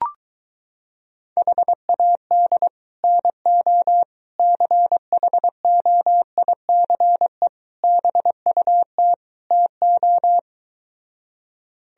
Morse code, heard as a keyed tone, about 23 wpm, 700 Hz. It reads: HAD NO CHOICE BUT TO